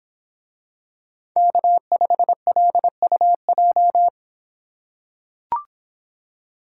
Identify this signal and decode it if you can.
Morse code, heard as a keyed tone, about 26 wpm, 700 Hz. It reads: K5LUJ